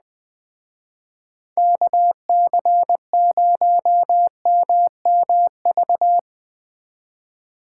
Morse code, heard as a keyed tone, about 20 wpm, 700 Hz. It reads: KC0MMV
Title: Speed Racing - USA 2x3 Call Signs 17wpm